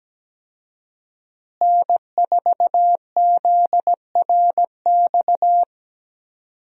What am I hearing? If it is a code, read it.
Morse code, heard as a keyed tone, about 17 wpm, 700 Hz. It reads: N4ZRX
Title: USA Call Signs - 5 Characters - Repeated 3x - 17wpm